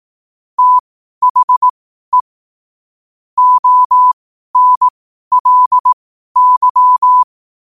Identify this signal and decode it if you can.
Morse code, heard as a keyed tone, about 18 wpm, spaced as for 12 wpm, 1000 Hz. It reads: THE ONLY
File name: qotdfiles / 12wpm